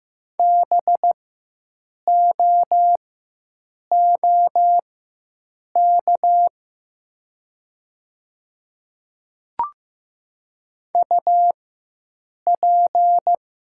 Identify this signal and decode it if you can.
Morse code, heard as a keyed tone, about 15 wpm, spaced as for 7 wpm, 700 Hz. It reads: BOOK UP
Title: 4x Character Spacing - Top 300 Words - 15wpm